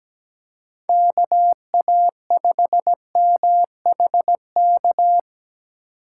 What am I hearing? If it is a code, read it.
Morse code, heard as a keyed tone, about 17 wpm, 700 Hz. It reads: KA5MHK
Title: USA Call Signs - 2x Character Spacing - 17wpm